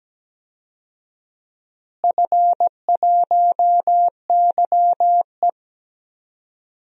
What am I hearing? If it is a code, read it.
Morse code, heard as a keyed tone, about 17 wpm, 700 Hz. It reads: F1YE
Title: DXCC-100 Call Signs - 4 Characters - Repeated 3x - 17wpm